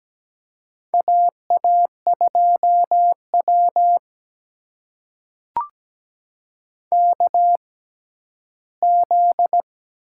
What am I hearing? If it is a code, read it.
Morse code, heard as a keyed tone, about 17 wpm, spaced as for 11 wpm, 700 Hz. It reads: AA2W KZ
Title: USA Call Signs - 4 Characters - 6x Character Spacing - 17wpm